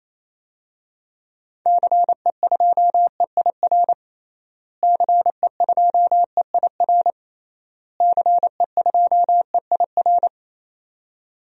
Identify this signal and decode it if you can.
Morse code, heard as a keyed tone, about 28 wpm, 700 Hz. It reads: CE2EIR CE2EIR CE2EIR
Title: DXCC-100 Call Signs - 6 Characters - Repeated 3x - 28wpm